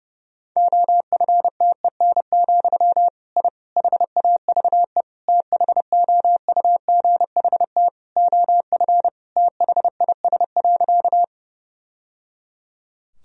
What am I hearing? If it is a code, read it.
Morse code, heard as a keyed tone, about 30 wpm, 700 Hz. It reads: OFTEN, I HAVE THOUGHT OF THIS.
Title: Sentences from Top 1000 Words - Review Part 1 - 30wpm